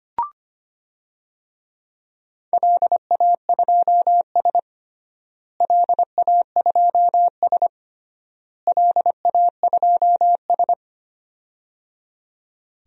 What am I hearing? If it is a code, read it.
Morse code, heard as a keyed tone, about 25 wpm, 700 Hz. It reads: LA2S LA2S LA2S
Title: DXCC-100 Call Signs - 4 Characters - Repeated 3x - 25wpm